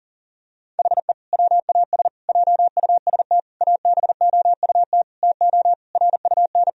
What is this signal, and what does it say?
Morse code, heard as a keyed tone, about 40 wpm, 700 Hz. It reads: HE WAS JUST ABOUT TO RUN
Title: Sentences from Top 1000 Words - Review Part 2 - 40wpm